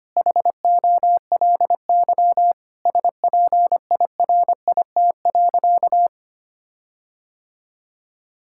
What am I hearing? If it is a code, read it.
Morse code, heard as a keyed tone, about 25 wpm, 700 Hz. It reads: HOLY SPIRIT.
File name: practice / 25wpm / Sentences with 9-Word Phrases - 25wpm